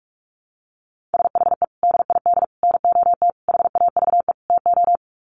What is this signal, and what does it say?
Morse code, heard as a keyed tone, about 44 wpm, 700 Hz. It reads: SHE DID NOT HAVE TO
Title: Sets of 5 Words Encore2 45wpm